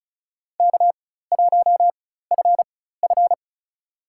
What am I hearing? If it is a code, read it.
Morse code, heard as a keyed tone, about 36 wpm, spaced as for 16 wpm, 700 Hz. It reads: K1FF
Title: USA Call Signs - 4 Characters - 4x Character Spacing - 35wpm